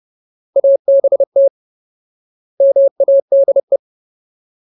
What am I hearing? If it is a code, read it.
Morse code, heard as a keyed tone, about 30 wpm, 550 Hz. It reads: ABT MADE
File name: Ultimate Word List / 30WPM Ultimate